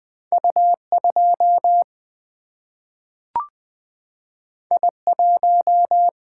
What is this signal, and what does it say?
Morse code, heard as a keyed tone, about 20 wpm, 700 Hz. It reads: U 2 I 1
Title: Single-Letter Single-Number 20wpm